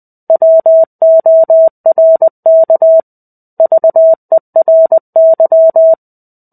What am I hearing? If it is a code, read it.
Morse code, heard as a keyed tone, about 20 wpm, 650 Hz. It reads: WORK VERY
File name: 05-Chapter-05 20 WPM